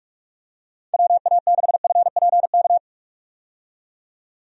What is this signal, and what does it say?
Morse code, heard as a keyed tone, about 46 wpm, 700 Hz. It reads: WA6FPK
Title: USA Call Signs - 6 Characters - 45wpm